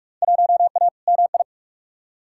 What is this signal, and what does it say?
Morse code, heard as a keyed tone, about 46 wpm, 700 Hz. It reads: 1A MI
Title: Field Day Exchanges - 45wpm